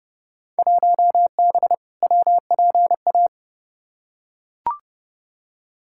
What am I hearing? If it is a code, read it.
Morse code, heard as a keyed tone, about 32 wpm, 700 Hz. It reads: 1B WPA
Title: Field Day Exchanges Encore - 30wpm